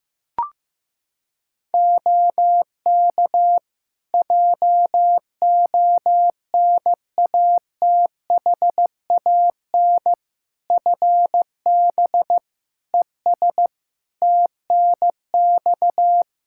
Morse code, heard as a keyed tone, about 15 wpm, 700 Hz. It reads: OK JONATHAN FB ES TNX
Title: Practice QSO - Exchange 2 - Info - 15wpm